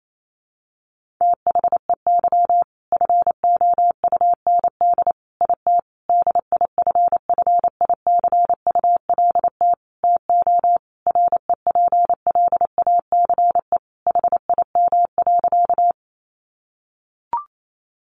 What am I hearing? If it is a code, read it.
Morse code, heard as a keyed tone, about 28 wpm, 700 Hz. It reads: THEY FOUND IT DIFFICULT TO REPLACE HIM.
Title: Sentences from Top 1400 Words 28wpm